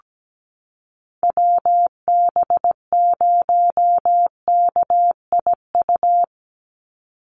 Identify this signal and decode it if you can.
Morse code, heard as a keyed tone, about 17 wpm, 700 Hz. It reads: WB0KIU
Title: NA SOTA Chaser Callsigns 17wpm